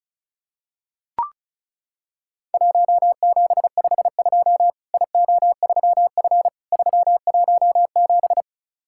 Morse code, heard as a keyed tone, about 35 wpm, 700 Hz. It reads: 1752 IO3F 317
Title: Worked All Europe DX Contest - QTC Exchange - 35wpm